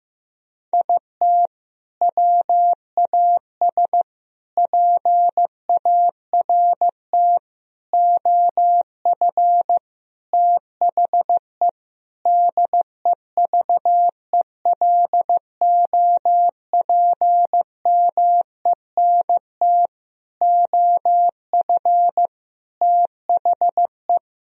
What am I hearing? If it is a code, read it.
Morse code, heard as a keyed tone, about 15 wpm, 700 Hz. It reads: IT WAS PART OF THE DEVELOPMENT OF THE
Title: Sentences from Top 1000 Words - Review Part 4 - 15wpm